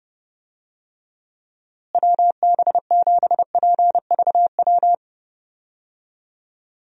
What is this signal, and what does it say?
Morse code, heard as a keyed tone, about 32 wpm, 700 Hz. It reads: WB7PVW
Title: USA Call Signs - 6 Characters - Repeated 2x - 30wpm